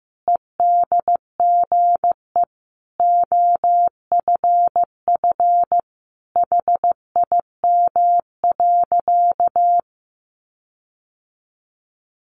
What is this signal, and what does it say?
Morse code, heard as a keyed tone, about 15 wpm, 700 Hz. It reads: EDGE OFF HIM.
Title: Sentences from Top 1300 Words 15wpm